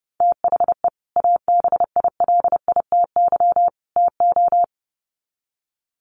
Morse code, heard as a keyed tone, about 30 wpm, 700 Hz. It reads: THE ABILITY TO
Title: Sets of 3 Words 30wpm